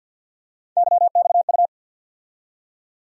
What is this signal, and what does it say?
Morse code, heard as a keyed tone, about 53 wpm, 700 Hz. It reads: YXU
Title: USA Call Sign Postfixes - 3 Letters 50wpm